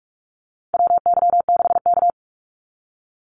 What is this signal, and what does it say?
Morse code, heard as a keyed tone, about 44 wpm, 700 Hz. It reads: WY6K
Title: Call Signs - 45wpm